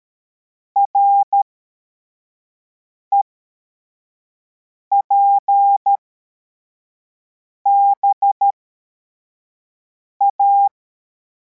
Morse code, heard as a keyed tone, about 13 wpm, spaced as for 4 wpm, 800 Hz. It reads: REPBA
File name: formation / cw / Text71